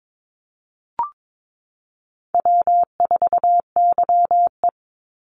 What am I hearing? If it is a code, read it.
Morse code, heard as a keyed tone, about 22 wpm, 690 Hz. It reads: W4YE
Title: Call Signs Repeated 3x - 22wpm